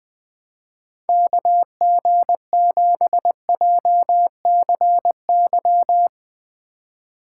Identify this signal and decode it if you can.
Morse code, heard as a keyed tone, about 20 wpm, 700 Hz. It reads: KG7JCY
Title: Speed Racing - USA Call Signs 17wpm